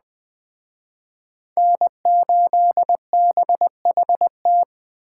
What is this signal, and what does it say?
Morse code, heard as a keyed tone, about 20 wpm, 700 Hz. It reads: N8BHT
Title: Speed Racing - USA Call Signs 17wpm